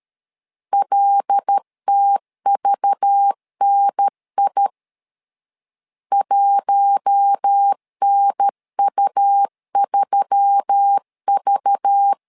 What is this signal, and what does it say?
Morse code, heard as a keyed tone, about 13 wpm, 800 Hz. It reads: LTVNI 1NU3V
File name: LEZIONI / 5 LBVC / 5 LBV 12wpm 1855